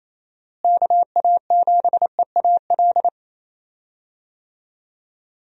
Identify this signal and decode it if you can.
Morse code, heard as a keyed tone, about 27 wpm, 700 Hz. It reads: KA7EAL
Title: USA Call Signs - 6 Characters - Repeated 3x - 28wpm